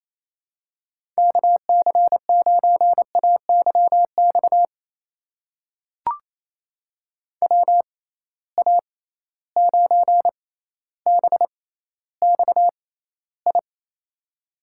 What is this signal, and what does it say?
Morse code, heard as a keyed tone, about 28 wpm, spaced as for 14 wpm, 700 Hz. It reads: KC9AYX WA9BXI
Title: USA Call Signs - 6 Characters - 6x Character Spacing - 28wpm